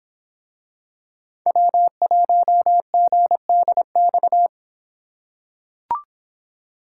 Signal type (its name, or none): Morse code